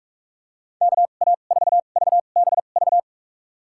Morse code, heard as a keyed tone, about 42 wpm, spaced as for 31 wpm, 700 Hz. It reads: KA4VBV
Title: USA Call Signs - 2x Character Spacing - 45wpm